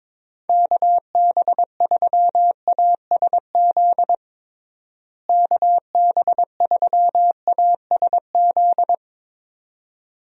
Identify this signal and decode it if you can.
Morse code, heard as a keyed tone, about 22 wpm, 700 Hz. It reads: KB3ASZ KB3ASZ
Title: USA Call Signs - 6 Characters - Repeated 2x - 22wpm